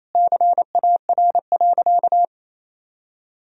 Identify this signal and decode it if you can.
Morse code, heard as a keyed tone, about 28 wpm, 700 Hz. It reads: CAR.
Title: Sentences from Top 1000 Words - Review Part 1 - 28wpm